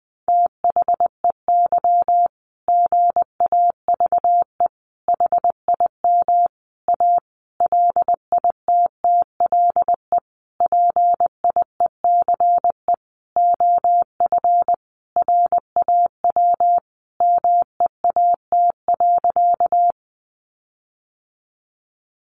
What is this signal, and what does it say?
Morse code, heard as a keyed tone, about 20 wpm, 700 Hz. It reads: THEY GAVE HIM A LITTLE PIECE OF RAW MEAT.
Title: Sentences from Top 1500 Words 20wpm